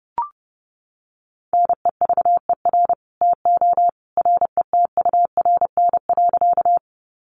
Morse code, heard as a keyed tone, about 30 wpm, 700 Hz. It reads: NEVER TO RETURN.
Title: Sentences from Top 800 Words 30wpm